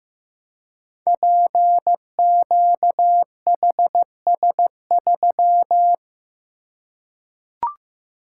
Morse code, heard as a keyed tone, about 15 wpm, 700 Hz. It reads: PQHS3
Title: Code Groups - 5 Letter:Numbers - 15wpm